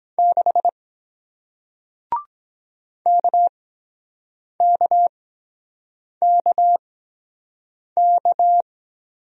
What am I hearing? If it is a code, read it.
Morse code, heard as a keyed tone, about 23 wpm, spaced as for 7 wpm, 700 Hz. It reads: 6 KKKK